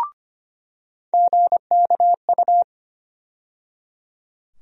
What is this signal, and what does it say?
Morse code, heard as a keyed tone, about 25 wpm, 700 Hz. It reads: GKU
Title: USA Call Sign Postfixes - 3 Letters 25wpm